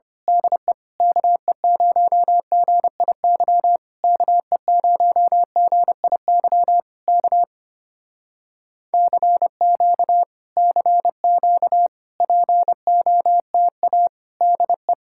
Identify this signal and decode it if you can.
Morse code, heard as a keyed tone, about 24 wpm, 700 Hz. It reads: DE KE0GIY KE0GIY K CQ CQ POTA DE
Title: Speed-Racing Common QSO Phrases - 25wpm